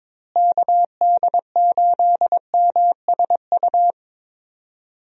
Morse code, heard as a keyed tone, about 22 wpm, 700 Hz. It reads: KD8MSU
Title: USA Call Signs - 6 Characters - Repeated 3x - 22wpm